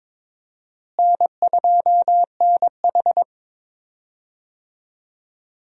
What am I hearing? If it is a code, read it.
Morse code, heard as a keyed tone, about 22 wpm, 700 Hz. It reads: N2NH
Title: USA Call Signs - 4 Characters - Repeated 2x - 22wpm